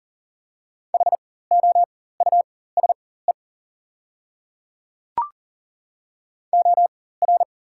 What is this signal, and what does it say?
Morse code, heard as a keyed tone, about 40 wpm, spaced as for 19 wpm, 690 Hz. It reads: HOUSE OR
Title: 4x Character Spacing - Top 300 Words - 40wpm